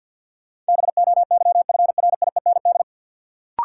Morse code, heard as a keyed tone, about 49 wpm, 700 Hz. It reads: BOYFRIEND
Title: Sentences from Top 1000 Words - Review Part 5 - New Word Drill - 50wpm